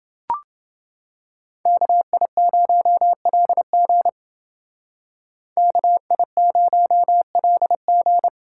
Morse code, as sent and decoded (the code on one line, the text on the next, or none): -.- .. ----- .-.. --. / -.- .. ----- .-.. --.
KI0LG KI0LG